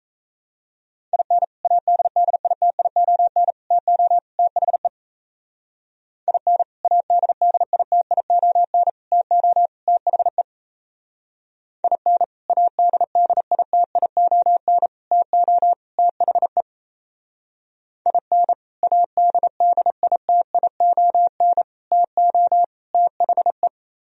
Morse code, heard as a keyed tone, about 34 wpm, 700 Hz. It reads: IN ADDITION TO THE IN ADDITION TO THE IN ADDITION TO THE IN ADDITION TO THE
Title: Speed-Racing - Sets of 4 Words 28wpm